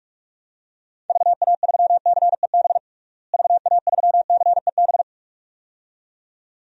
Morse code, heard as a keyed tone, about 41 wpm, 700 Hz. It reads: VA3CEB VA3CEB
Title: DXCC-100 Call Signs - 6 Characters - Repeated 2x - 45wpm